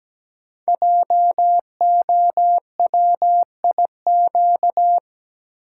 Morse code, heard as a keyed tone, about 17 wpm, 700 Hz. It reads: JOWIQ
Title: Code Groups - 5 Letters - 17wpm